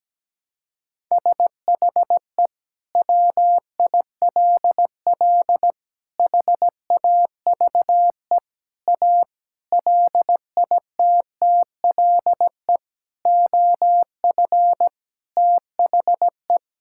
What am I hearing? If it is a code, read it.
Morse code, heard as a keyed tone, about 17 wpm, 700 Hz. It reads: SHE WILL HAVE A LITTLE OF THE
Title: Sentences from Top 1000 Words - Review Part 5 - 17wpm